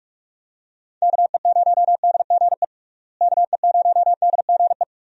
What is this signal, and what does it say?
Morse code, heard as a keyed tone, about 46 wpm, 700 Hz. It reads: KE0DGE KE0DGE
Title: USA Call Signs - 6 Characters - Repeated 2x - 45wpm